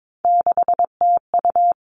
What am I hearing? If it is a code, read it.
Morse code, heard as a keyed tone, about 22 wpm, 700 Hz. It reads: 6TU
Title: All Cut Numbers - 3-Digit 22wpm